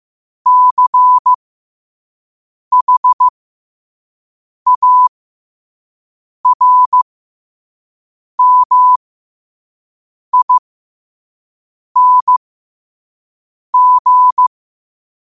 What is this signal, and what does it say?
Morse code, heard as a keyed tone, about 15 wpm, spaced as for 5 wpm, 1000 Hz. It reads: CHARMING
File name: qotdfiles / 5wpm